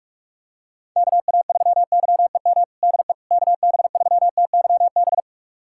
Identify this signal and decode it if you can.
Morse code, heard as a keyed tone, about 45 wpm, 700 Hz. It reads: KA3YEM DE KB3TYB
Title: USA Callsign DE Callsign 45wpm